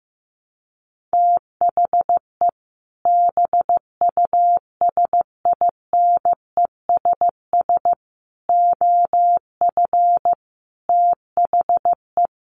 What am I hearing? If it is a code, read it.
Morse code, heard as a keyed tone, about 15 wpm, 700 Hz. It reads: THE BUSINESS OF THE
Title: Sentences from Top 1500 Words 15wpm